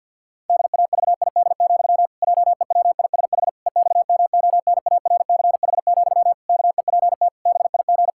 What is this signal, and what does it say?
Morse code, heard as a keyed tone, about 51 wpm, 700 Hz. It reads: DAVID, JEWISH EXMONARCH, KEPT BIG
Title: Pangram Challenge v2 - 50wpm